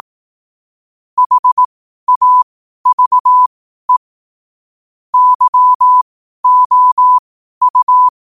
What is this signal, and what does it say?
Morse code, heard as a keyed tone, about 18 wpm, spaced as for 12 wpm, 1000 Hz. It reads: HAVE YOU
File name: qotdfiles / 12wpm